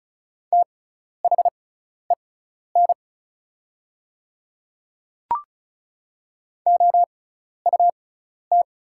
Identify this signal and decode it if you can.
Morse code, heard as a keyed tone, about 36 wpm, spaced as for 12 wpm, 700 Hz. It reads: THEN OUT
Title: 6x Character Spacing - Top 100 Words - 35wpm